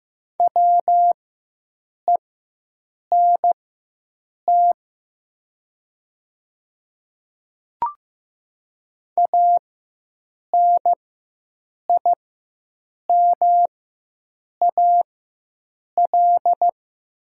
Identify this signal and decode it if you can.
Morse code, heard as a keyed tone, about 15 wpm, spaced as for 7 wpm, 700 Hz. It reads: WENT ANIMAL
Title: 4x Character Spacing - Top 300 Words - 15wpm